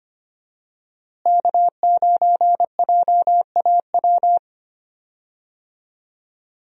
Morse code, heard as a keyed tone, about 25 wpm, 700 Hz. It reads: K9JAW
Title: USA Call Signs - 5 Characters - Repeated 3x - 25wpm